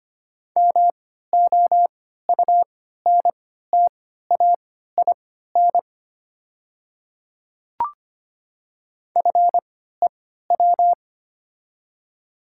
Morse code, heard as a keyed tone, about 25 wpm, spaced as for 14 wpm, 690 Hz. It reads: MOUNTAIN FEW
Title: 3x Character Spacing - Top 300 Words - 25wpm